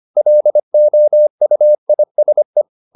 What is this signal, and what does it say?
Morse code, heard as a keyed tone, about 25 wpm, 600 Hz. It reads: LOUISE
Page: Ferie 25 wpm